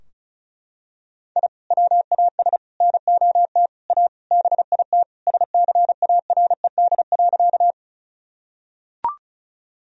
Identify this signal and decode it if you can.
Morse code, heard as a keyed tone, about 36 wpm, 700 Hz. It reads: I WAS NOT A BIT SCARED.
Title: Sentences from Top 1000 Words - Review Part 4 - 35wpm